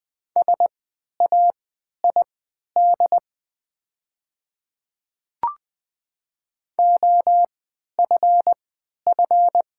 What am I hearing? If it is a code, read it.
Morse code, heard as a keyed tone, about 20 wpm, spaced as for 11 wpm, 690 Hz. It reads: SAID OFF
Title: 3x Character Spacing - Top 200 Words - 20wpm